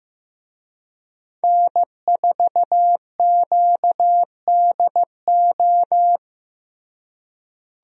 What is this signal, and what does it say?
Morse code, heard as a keyed tone, about 15 wpm, 700 Hz. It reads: N4QDO
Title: USA Call Signs - 5 Characters - Repeated 2x - 15wpm